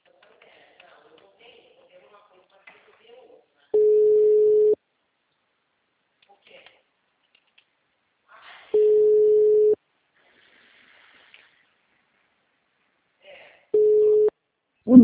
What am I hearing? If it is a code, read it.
Ringback tone (Europe, 425 Hz), with speech.
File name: out- -2020- - - .88418